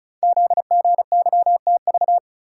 Morse code, heard as a keyed tone, about 36 wpm, 700 Hz. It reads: ZGYTV
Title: Code Groups - 5 Letters - 35wpm